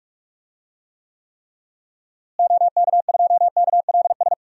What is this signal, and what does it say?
Morse code, heard as a keyed tone, about 43 wpm, 700 Hz. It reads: OK2KLS
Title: DXCC-100 Call Signs - 6 Characters - Repeated 3x - 45wpm